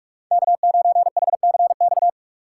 Morse code, heard as a keyed tone, about 46 wpm, 700 Hz. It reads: K9HCX
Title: USA Call Signs - 5 Characters - 2x Character Spacing - 45wpm